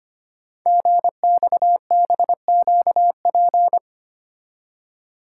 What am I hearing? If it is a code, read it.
Morse code, heard as a keyed tone, about 25 wpm, 700 Hz. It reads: GXBQP
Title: Code Groups - 5 Letters - 25wpm